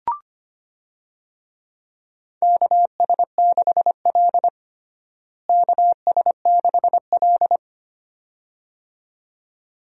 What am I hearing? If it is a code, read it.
Morse code, heard as a keyed tone, about 25 wpm, 700 Hz. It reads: KS6L KS6L
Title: USA Call Signs - 4 Characters - Repeated 2x - 25wpm